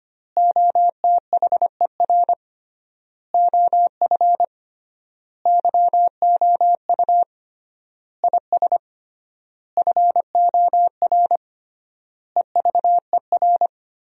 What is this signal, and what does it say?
Morse code, heard as a keyed tone, about 25 wpm, 700 Hz. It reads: OTHER OF YOU IS FOR EVER
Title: 3x Word-Spacing - Sentences from Top 400 Words - 25wpm